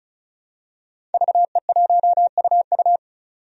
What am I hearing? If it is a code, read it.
Morse code, heard as a keyed tone, about 34 wpm, 700 Hz. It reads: VE1UU
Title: DXCC-100 Call Signs - 5 Characters - 35wpm